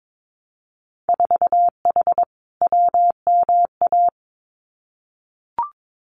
Morse code, heard as a keyed tone, about 22 wpm, 700 Hz. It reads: 4H WMA
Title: Winter Field Day Exchanges Encore 22wpm